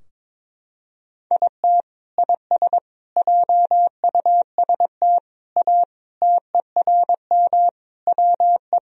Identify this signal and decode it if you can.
Morse code, heard as a keyed tone, about 22 wpm, 700 Hz. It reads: IT IS JUST A TERM WE
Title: Sentences from Top 1000 Words - Review Part 4 - 22wpm